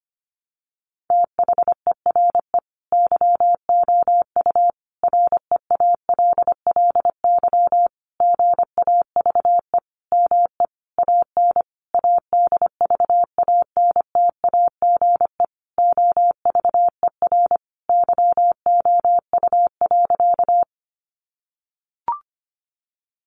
Morse code, heard as a keyed tone, about 25 wpm, 700 Hz. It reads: THERE YOU REALLY GAVE ME AN ADVANTAGE OVER YOU.
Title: Sentences from Top 900 Words 25wpm